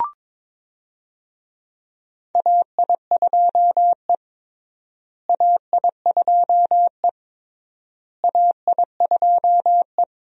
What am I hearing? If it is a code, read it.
Morse code, heard as a keyed tone, about 22 wpm, 700 Hz. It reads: AI2E AI2E AI2E
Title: USA Call Signs - 4 Characters - Repeated 3x - 22wpm